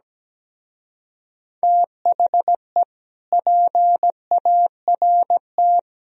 Morse code, heard as a keyed tone, about 17 wpm, 700 Hz. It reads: THE PART
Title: Speed-Racing - Sets of 4 Words 17wpm